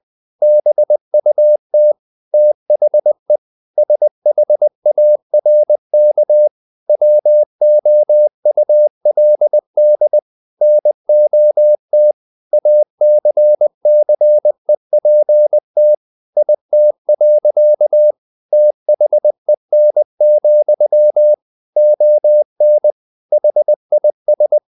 Morse code, heard as a keyed tone, about 20 wpm, 600 Hz. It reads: BUT THE SHARK WOULD NOT ACCEPT IT. THEN, ON HIS
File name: the old man and the sea part 2 20wpm